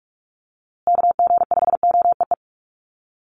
Morse code, heard as a keyed tone, about 44 wpm, 700 Hz. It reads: KG5OEE
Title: USA Call Signs Repeated 2x 45wpm